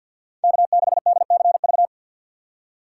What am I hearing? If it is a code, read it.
Morse code, heard as a keyed tone, about 48 wpm, 700 Hz. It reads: K6DCV
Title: USA Call Signs - 3x Character Spacing - 50wpm